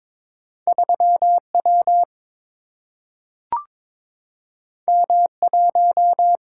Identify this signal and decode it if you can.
Morse code, heard as a keyed tone, about 22 wpm, 700 Hz. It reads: 3W M1
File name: practice / 22wpm / Two Letter-Number Combinations - 22wpm